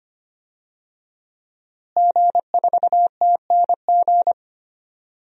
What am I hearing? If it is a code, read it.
Morse code, heard as a keyed tone, about 25 wpm, 700 Hz. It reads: G4TNG
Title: DXCC-100 Call Signs - 5 Characters - Repeated 2x - 25wpm